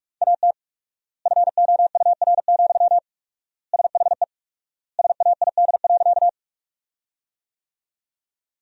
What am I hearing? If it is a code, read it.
Morse code, heard as a keyed tone, about 46 wpm, 700 Hz. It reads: AT FOUR, SHE SAID.
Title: 4x Word-Spacing - Sentences from Top 300 Words - 45wpm